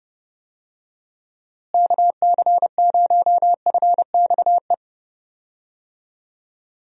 Morse code, heard as a keyed tone, about 30 wpm, 700 Hz. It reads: KC0FXE
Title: USA Call Signs - 6 Characters - Repeated 2x - 30wpm